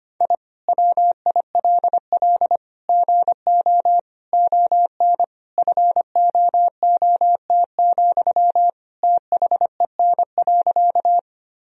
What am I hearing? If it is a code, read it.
Morse code, heard as a keyed tone, about 25 wpm, 700 Hz. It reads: I WILL GO ON FOOT, THEN.
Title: Sentences from Top 1000 Words - Review Part 3 - 25wpm